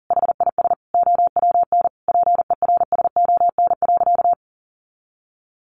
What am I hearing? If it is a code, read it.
Morse code, heard as a keyed tone, about 40 wpm, 700 Hz. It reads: HIS OWN PERSON.
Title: Sentences from Top 600 Words 40wpm